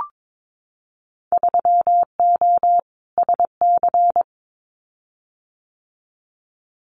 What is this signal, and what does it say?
Morse code, heard as a keyed tone, about 22 wpm, 700 Hz. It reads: 3O SC
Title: Winter Field Day Exchanges Encore 22wpm